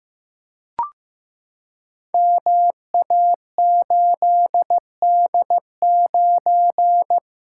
Morse code, heard as a keyed tone, about 15 wpm, 700 Hz. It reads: MA8D9
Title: Code Groups - 5 Letter:Numbers - 15wpm